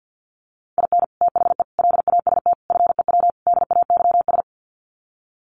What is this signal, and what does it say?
Morse code, heard as a keyed tone, about 50 wpm, 700 Hz. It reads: IN THE LAST FEW DAYS
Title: Sets of 5 Words Encore2 50wpm